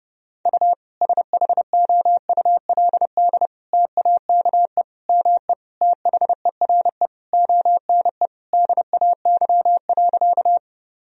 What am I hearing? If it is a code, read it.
Morse code, heard as a keyed tone, about 30 wpm, 700 Hz. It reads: U SHOULD TAKE ME THERE ONE DAY.
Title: Sentences from Top 300 Words and Abbreviations 30wpm